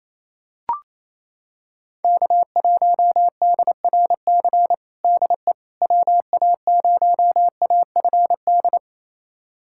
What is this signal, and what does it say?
Morse code, heard as a keyed tone, about 28 wpm, 700 Hz. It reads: K1DRC DE WA0AFD
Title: USA Callsign DE Callsign 28wpm